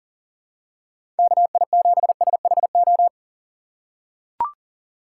Morse code, heard as a keyed tone, about 40 wpm, 700 Hz. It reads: KI7SHO
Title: USA Call Signs - 6 Characters - 4x Character Spacing - 40wpm